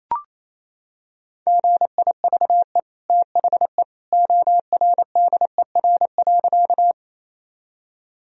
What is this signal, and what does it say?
Morse code, heard as a keyed tone, about 27 wpm, 700 Hz. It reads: GIVE THE ORDER.
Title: Sentences from Top 1000 Words - Review Part 2 - 28wpm